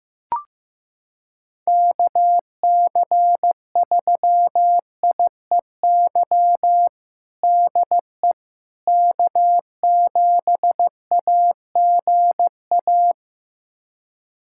Morse code, heard as a keyed tone, about 15 wpm, 700 Hz. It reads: KC3IEY DE K7AGA
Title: USA Callsign DE Callsign 15wpm